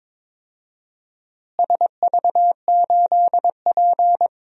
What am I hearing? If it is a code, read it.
Morse code, heard as a keyed tone, about 22 wpm, 700 Hz. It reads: SV8P
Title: DXCC-100 Call Signs - 4 Characters - Repeated 2x - 22wpm